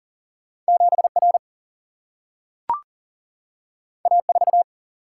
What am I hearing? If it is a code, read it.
Morse code, heard as a keyed tone, about 40 wpm, 700 Hz. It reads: 7R A4
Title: Single-Letter Single-Number 40wpm